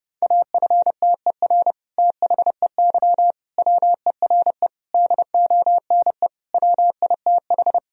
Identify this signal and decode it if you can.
Morse code, heard as a keyed tone, about 32 wpm, 700 Hz. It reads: AFTER THEY WERE DONE WITH
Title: Sentences from Top 1000 Words - Review Part 2 - 30wpm